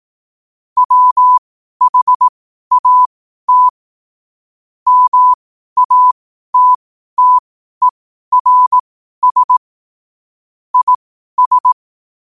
Morse code, heard as a keyed tone, about 18 wpm, spaced as for 12 wpm, 1000 Hz. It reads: WHAT MATTERS IS
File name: qotdfiles / 12wpm